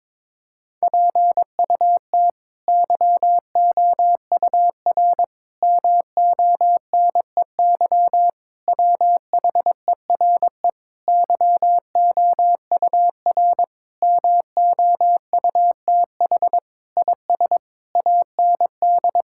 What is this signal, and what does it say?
Morse code, heard as a keyed tone, about 22 wpm, 700 Hz. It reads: PUT YOUR MONEY WHERE YOUR MOUTH IS AND
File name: practice / 22wpm / Sentences with 8-Word Phrases - 22wpm